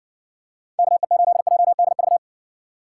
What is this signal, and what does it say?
Morse code, heard as a keyed tone, about 69 wpm, 700 Hz. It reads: KE8JDV